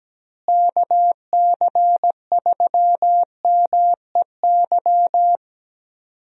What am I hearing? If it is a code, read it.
Morse code, heard as a keyed tone, about 17 wpm, 700 Hz. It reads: KC3MEY
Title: USA Call Signs - 6 Characters - Repeated 3x - 17wpm